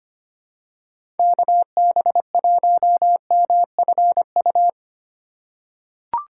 Morse code, heard as a keyed tone, about 25 wpm, 700 Hz. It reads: KB1MFU
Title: USA Call Signs - 6 Characters - 2x Character Spacing - 25wpm